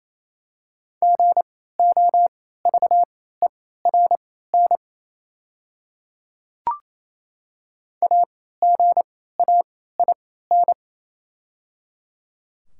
Morse code, heard as a keyed tone, about 28 wpm, spaced as for 16 wpm, 700 Hz. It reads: GOVERN AGAIN
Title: 3x Character Spacing - Top 500 Words - 28wpm